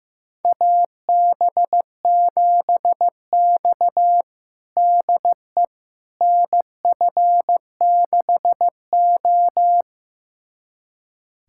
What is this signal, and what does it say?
Morse code, heard as a keyed tone, about 15 wpm, 700 Hz. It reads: AB7X DE NF6O
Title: USA Callsign DE Callsign - 4 Characters - 15wpm